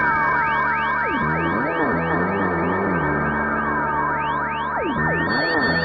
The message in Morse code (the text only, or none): none